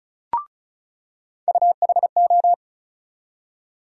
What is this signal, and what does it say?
Morse code, heard as a keyed tone, about 34 wpm, 700 Hz. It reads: UHO
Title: USA Call Sign Postfixes - 3 Letters 35wpm